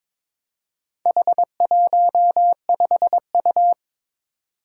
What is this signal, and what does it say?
Morse code, heard as a keyed tone, about 22 wpm, 700 Hz. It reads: H15U
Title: Four Challenging Letter-Number Combinations 22wpm